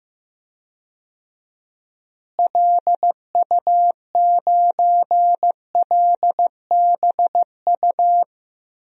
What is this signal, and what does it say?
Morse code, heard as a keyed tone, about 15 wpm, 700 Hz. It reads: LU9LBU
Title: DXCC-100 Call Signs - 6 Characters - Repeated 2x - 15wpm